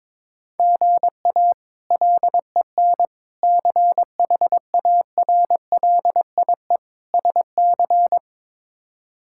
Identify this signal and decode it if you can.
Morse code, heard as a keyed tone, about 22 wpm, 700 Hz. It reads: GA LEN CHARLIE SC
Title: K1USN SST QSO Practice - 22wpm